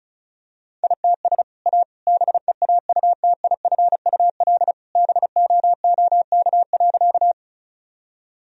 Morse code, heard as a keyed tone, about 36 wpm, 700 Hz. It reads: ITS A BEAUTIFUL BOOK.
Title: Sentences from Top 1000 Words - Review Part 1 - 35wpm